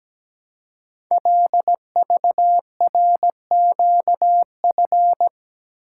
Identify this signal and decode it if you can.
Morse code, heard as a keyed tone, about 17 wpm, 700 Hz. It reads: LVRQF
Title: Code Groups - 5 Letters - 17wpm